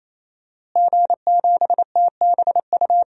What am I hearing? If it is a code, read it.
Morse code, heard as a keyed tone, about 28 wpm, 690 Hz. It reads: G7TBU
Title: DXCC-100 Call Signs - 5 Characters - Repeated 3x - 28wpm